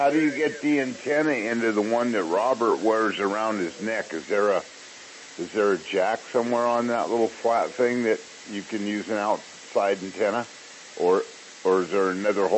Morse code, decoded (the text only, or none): none